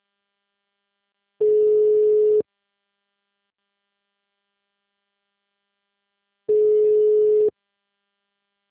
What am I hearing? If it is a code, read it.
Ringback tone (Europe, 425 Hz).